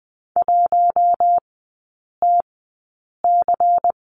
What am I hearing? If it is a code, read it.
Morse code, heard as a keyed tone, about 20 wpm, spaced as for 8 wpm, 700 Hz. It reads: 1TC
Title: 145 Lesson 24e - Letter y - Three Characters 20wpm